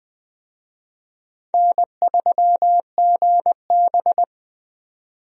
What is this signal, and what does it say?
Morse code, heard as a keyed tone, about 20 wpm, 700 Hz. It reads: N3GB
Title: USA Call Signs - 4 Characters - 20wpm